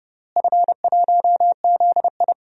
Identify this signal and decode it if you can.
Morse code, heard as a keyed tone, about 30 wpm, 700 Hz. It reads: F1ZI
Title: DXCC-100 Call Signs - 4 Characters - Repeated 3x - 30wpm